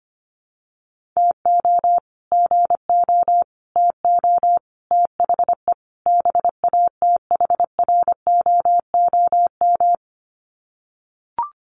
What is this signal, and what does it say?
Morse code, heard as a keyed tone, about 25 wpm, 700 Hz. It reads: TO GO TO THE BATHROOM
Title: Sets of 5 Words Encore 25wpm